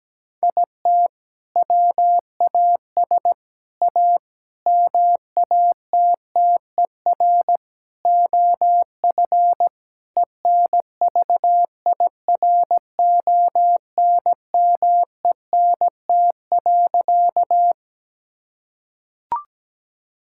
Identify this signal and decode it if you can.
Morse code, heard as a keyed tone, about 17 wpm, 700 Hz. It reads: IT WAS A MATTER OF ENVIRONMENT.
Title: Sentences from Top 1000 Words - Review Part 5 - 17wpm